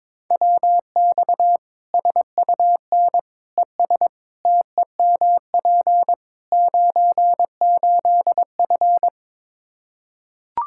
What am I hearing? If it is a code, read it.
Morse code, heard as a keyed tone, about 22 wpm, 700 Hz. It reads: WX SUN ES TEMP 98F
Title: K1USN SST QSO Practice 22wpm